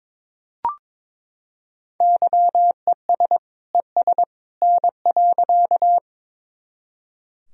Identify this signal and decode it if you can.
Morse code, heard as a keyed tone, about 22 wpm, 700 Hz. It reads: YES ES N.
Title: Sentences from Top 500 Words and Abbreviations 22wpm